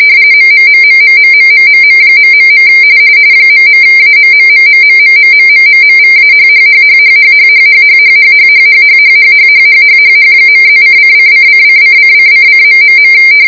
RTTY, about 45.45 baud. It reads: MMMMMMMMMM... ..AMMMMMMMMMMMVAHHHHHHHHHHHHHHHHHHHHHHHH.+:::::::::::...